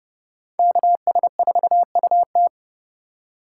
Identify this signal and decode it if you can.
Morse code, heard as a keyed tone, about 29 wpm, 700 Hz. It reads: KS4UT
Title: USA Call Signs - 5 Characters - 3x Character Spacing - 30wpm